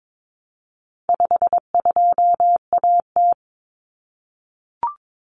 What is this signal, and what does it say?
Morse code, heard as a keyed tone, about 22 wpm, 700 Hz. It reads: 52AT
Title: Common Cut Numbers - 4-Digit 22wpm